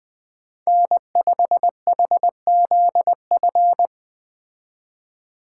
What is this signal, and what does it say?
Morse code, heard as a keyed tone, about 20 wpm, 700 Hz. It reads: N5HZF
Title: Speed Racing - USA 1x3 Call Signs 17wpm